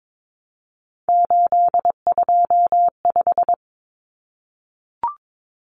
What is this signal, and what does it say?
Morse code, heard as a keyed tone, about 22 wpm, 700 Hz. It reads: 825
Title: Common Cut Numbers - 3-Digit 22wpm